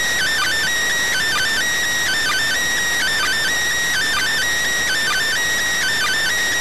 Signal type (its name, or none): Emergency Alert System (SAME) tones